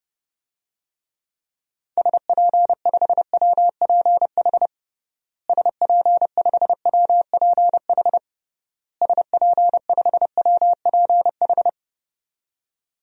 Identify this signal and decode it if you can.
Morse code, heard as a keyed tone, about 32 wpm, 700 Hz. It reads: SP5WPH SP5WPH SP5WPH
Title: DXCC-100 Call Signs - 6 Characters - Repeated 3x - 30wpm